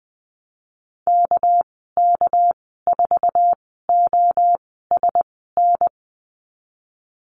Morse code, heard as a keyed tone, about 19 wpm, spaced as for 14 wpm, 700 Hz. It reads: KK4OSN
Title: 231 Lesson 37g - Letter x - Call Signs 20wpm